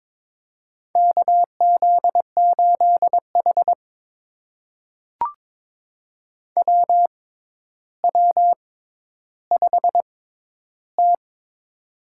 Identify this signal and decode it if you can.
Morse code, heard as a keyed tone, about 22 wpm, spaced as for 11 wpm, 700 Hz. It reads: KZ8H WW5T
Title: USA Call Signs - 4 Characters - 6x Character Spacing - 22wpm